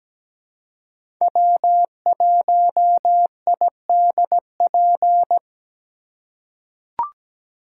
Morse code, heard as a keyed tone, about 17 wpm, 700 Hz. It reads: W1IDP
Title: USA Call Signs - 5 Characters - 2x Character Spacing - 17wpm